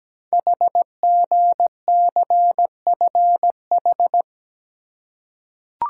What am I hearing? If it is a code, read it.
Morse code, heard as a keyed tone, about 17 wpm, 700 Hz. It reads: HGCFH
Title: Code Groups - 5 Letters - 17wpm